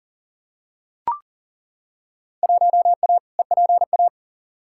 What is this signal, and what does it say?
Morse code, heard as a keyed tone, about 40 wpm, 700 Hz. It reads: 1A EPA
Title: Field Day Exchanges - 40wpm